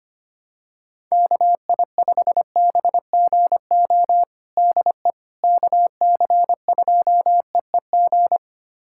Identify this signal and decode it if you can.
Morse code, heard as a keyed tone, about 25 wpm, 700 Hz. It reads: KI5BGO DE KC2EEG
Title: USA Callsign DE Callsign 25wpm